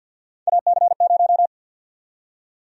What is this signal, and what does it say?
Morse code, heard as a keyed tone, about 52 wpm, 700 Hz. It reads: AC0
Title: USA Call Sign Prefixes - 50wpm